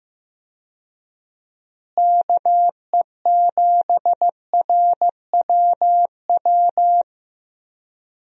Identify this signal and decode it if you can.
Morse code, heard as a keyed tone, about 15 wpm, 700 Hz. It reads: KE7RWW
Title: USA Call Signs - 6 Characters - Repeated 2x - 15wpm